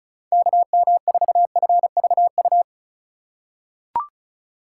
Morse code, heard as a keyed tone, about 36 wpm, 700 Hz. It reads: KM4FVU
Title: USA Call Signs - 6 Characters - 3x Character Spacing - 35wpm